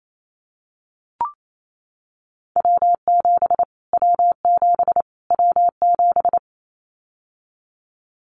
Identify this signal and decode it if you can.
Morse code, heard as a keyed tone, about 28 wpm, 700 Hz. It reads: W7 W7 W7
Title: Callsign Prefixes for Top 25 DXCC - Repeated 3x - 28wpm